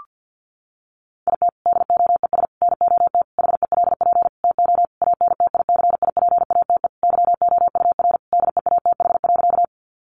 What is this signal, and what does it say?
Morse code, heard as a keyed tone, about 50 wpm, 700 Hz. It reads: IT DOES NOT HELP TO ANTICIPATE YOUR DEATH.
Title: Sentences from Top 1500 Words 50wpm